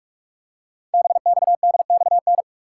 Morse code, heard as a keyed tone, about 44 wpm, 700 Hz. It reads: BXDXN
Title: Code Groups - 5 Letter:Numbers - 45wpm